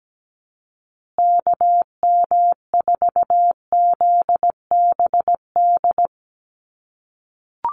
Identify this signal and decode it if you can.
Morse code, heard as a keyed tone, about 17 wpm, 700 Hz. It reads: KM4ZBD
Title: USA Call Signs Repeated 3x 17wpm